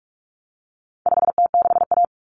Morse code, heard as a keyed tone, about 44 wpm, 700 Hz. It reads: 5T6A
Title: Common Cut Numbers - 4-Digit 45wpm